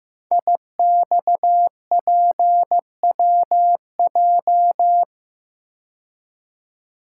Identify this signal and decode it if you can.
Morse code, heard as a keyed tone, about 15 wpm, 700 Hz. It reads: IXPWJ
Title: Code Groups - 5 Letters - 15wpm